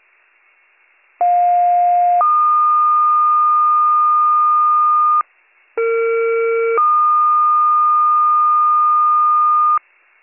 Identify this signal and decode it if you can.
Two-tone page, heard as a one-second tone, then a three-second tone, about 710 then 1220 Hz, about 460 then 1220 Hz. It is heard from 1.2 to 9.8 s.